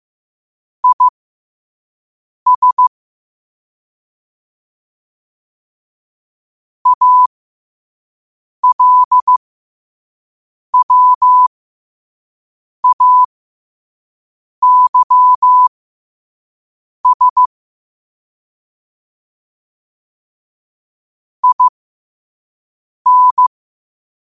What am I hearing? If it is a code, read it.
Morse code, heard as a keyed tone, about 15 wpm, spaced as for 5 wpm, 1000 Hz. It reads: IS ALWAYS IN